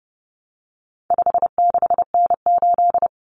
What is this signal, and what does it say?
Morse code, heard as a keyed tone, about 30 wpm, 700 Hz. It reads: 56N8
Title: Common Cut Numbers - 4-Digit 30wpm